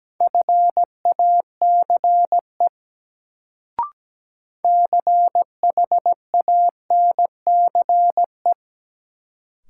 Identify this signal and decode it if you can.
Morse code, heard as a keyed tone, about 17 wpm, 700 Hz. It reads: FACE CHANCE
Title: Sets of 3 Words - New Word Drill - 17wpm